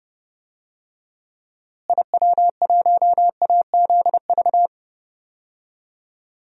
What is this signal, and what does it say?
Morse code, heard as a keyed tone, about 30 wpm, 700 Hz. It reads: IW1AZV
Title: DXCC-100 Call Signs - 6 Characters - Repeated 3x - 30wpm